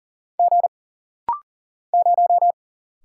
Morse code, heard as a keyed tone, about 40 wpm, 700 Hz. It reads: G 0
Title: Single Letter-Number - Rapid-Fire 40wpm